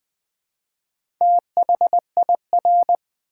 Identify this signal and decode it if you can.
Morse code, heard as a keyed tone, about 20 wpm, 700 Hz. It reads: THIR
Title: Build A Word - 1 Letter At A Time - 5-Letter Words - 20wpm